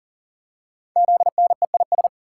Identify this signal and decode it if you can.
Morse code, heard as a keyed tone, about 40 wpm, 700 Hz. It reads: ZNEIS
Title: Code Groups - 5 Letters - 40wpm